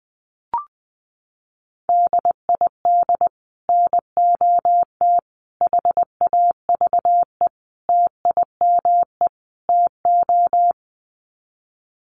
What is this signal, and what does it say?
Morse code, heard as a keyed tone, about 20 wpm, 700 Hz. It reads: DID NOT HAVE TIME TO
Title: Sets of 5 Words Encore 20wpm